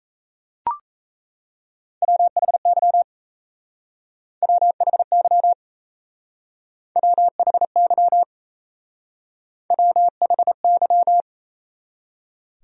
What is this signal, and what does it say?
Morse code, heard as a keyed tone, about 37 wpm, 700 Hz. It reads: WHY WHY WHY WHY